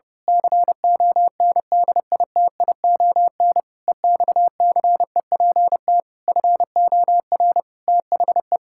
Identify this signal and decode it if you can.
Morse code, heard as a keyed tone, about 30 wpm, 700 Hz. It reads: CONDITION EXCEPT FOR THE
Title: Pangram Challenge v2 - 30wpm